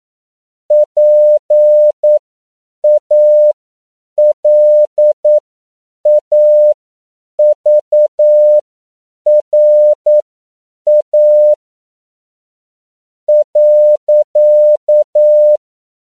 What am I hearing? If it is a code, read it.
Morse code, heard as a keyed tone, about 9 wpm, spaced as for 7 wpm, 600 Hz. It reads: PALAVRA .